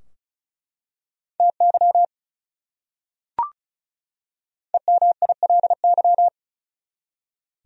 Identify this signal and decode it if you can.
Morse code, heard as a keyed tone, about 36 wpm, 700 Hz. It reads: TY EMILY
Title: USA Common Names - Max 5 Letters - 35wpm